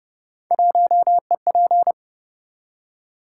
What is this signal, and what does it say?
Morse code, heard as a keyed tone, about 28 wpm, 700 Hz. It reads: 1EP